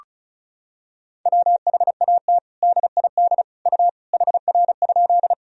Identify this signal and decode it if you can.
Morse code, heard as a keyed tone, about 36 wpm, 700 Hz. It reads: WHAT DID U HR?
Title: Sentences from Top 300 Words and Abbreviations 35wpm